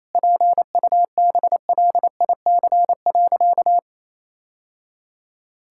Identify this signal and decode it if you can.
Morse code, heard as a keyed tone, about 28 wpm, 700 Hz. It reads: PUBLIC.
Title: Sentences from Top 1000 Words - Review Part 5 - 28wpm